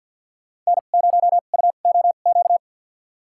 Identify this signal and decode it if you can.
Morse code, heard as a keyed tone, about 53 wpm, spaced as for 37 wpm, 700 Hz. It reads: N0UOQ
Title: USA Call Signs - 5 Characters - 2x Character Spacing - 50wpm